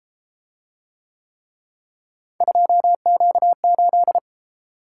Morse code, heard as a keyed tone, about 32 wpm, 700 Hz. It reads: 2Q8